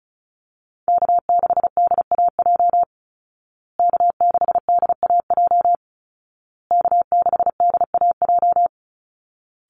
Morse code, heard as a keyed tone, about 34 wpm, 700 Hz. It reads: K6DAJ K6DAJ K6DAJ
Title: Call Signs Repeated 2x - 35wpm